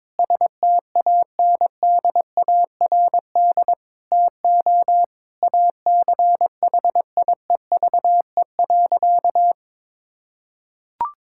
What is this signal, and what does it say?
Morse code, heard as a keyed tone, about 22 wpm, 700 Hz. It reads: STANDARD TO ACHIEVE.
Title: Sentences from Top 1000 Words - Review Part 6 - 22wpm